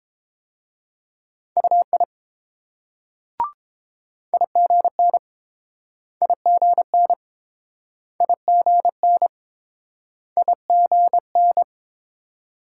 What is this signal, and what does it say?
Morse code, heard as a keyed tone, about 28 wpm, 700 Hz. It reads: UI IGN IGN IGN IGN